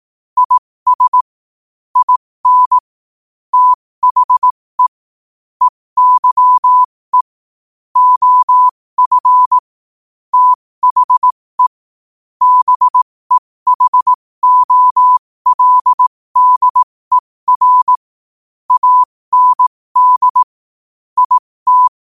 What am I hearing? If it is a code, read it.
Morse code, heard as a keyed tone, about 18 wpm, spaced as for 15 wpm, 1000 Hz. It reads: IS IN THE EYE OF THE BEHOLDER AND IT
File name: qotdfiles / 15wpm